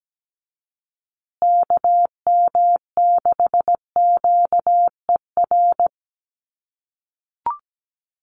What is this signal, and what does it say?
Morse code, heard as a keyed tone, about 17 wpm, 700 Hz. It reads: KM6QER
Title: USA Call Signs Repeated 3x 17wpm